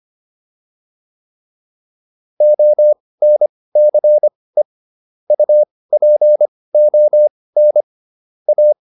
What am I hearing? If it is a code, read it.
Morse code, heard as a keyed tone, about 25 wpm, spaced as for 18 wpm, 600 Hz. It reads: ONCE UPON A